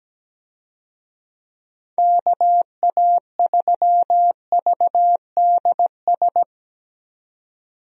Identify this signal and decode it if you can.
Morse code, heard as a keyed tone, about 17 wpm, 700 Hz. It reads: KA3VDS
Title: USA Call Signs - 6 Characters - 17wpm